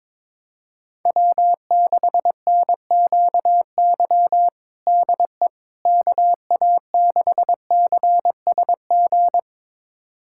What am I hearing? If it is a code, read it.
Morse code, heard as a keyed tone, about 22 wpm, 700 Hz. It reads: W6NQY DE KA6CSG
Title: USA Callsign DE Callsign 22wpm